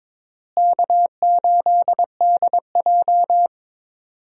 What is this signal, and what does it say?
Morse code, heard as a keyed tone, about 22 wpm, 700 Hz. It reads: K8DJ
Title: USA Call Signs - 4 Characters - Repeated 2x - 22wpm